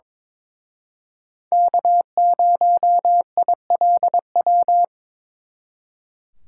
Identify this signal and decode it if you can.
Morse code, heard as a keyed tone, about 26 wpm, 700 Hz. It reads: K0ILW
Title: Speed Racing - USA Call Signs 22wpm